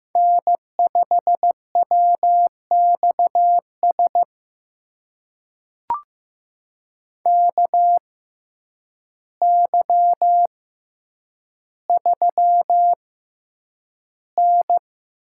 Morse code, heard as a keyed tone, about 15 wpm, spaced as for 8 wpm, 700 Hz. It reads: N5WXS KY3N
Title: USA Call Signs - 6x Character Spacing - 15wpm